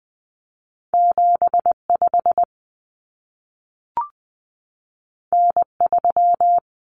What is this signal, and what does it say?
Morse code, heard as a keyed tone, about 20 wpm, 700 Hz. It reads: 75 N3
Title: Common Cut Numbers - 2-Digit 20wpm